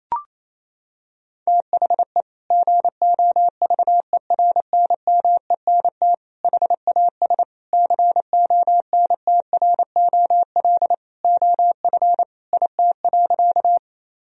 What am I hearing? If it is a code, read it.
Morse code, heard as a keyed tone, about 28 wpm, 700 Hz. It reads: THE GOVERNMENT HAS CONTROL OF IT.
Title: Sentences from Top 1000 Words - Review Part 3 - 28wpm